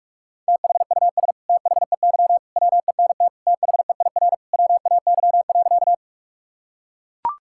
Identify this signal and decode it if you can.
Morse code, heard as a keyed tone, about 42 wpm, 690 Hz. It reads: THUS THEY WENT THEIR WAY.
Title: Sentences from Top 1000 Words - Review Part 5 - 45wpm